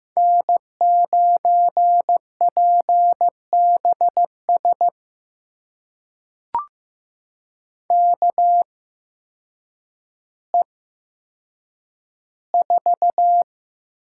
Morse code, heard as a keyed tone, about 15 wpm, spaced as for 8 wpm, 700 Hz. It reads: N9PBS KE4
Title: USA Call Signs - 8x Character Spacing - 15wpm